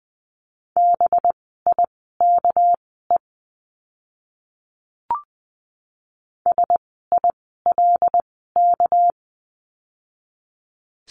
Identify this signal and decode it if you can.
Morse code, heard as a keyed tone, about 19 wpm, spaced as for 14 wpm, 700 Hz. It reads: BIKE SILK
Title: 199 Lesson 32f - Letter k - Words 20wpm